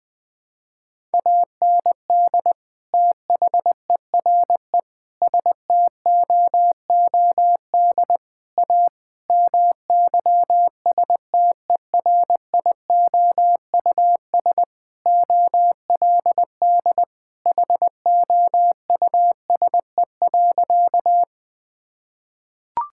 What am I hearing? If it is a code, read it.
Morse code, heard as a keyed tone, about 21 wpm, 700 Hz. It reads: AND THERE STOOD A MYSTERIOUS OLD HOUSE.
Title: Sentences with 2-Word Phrases - 20wpm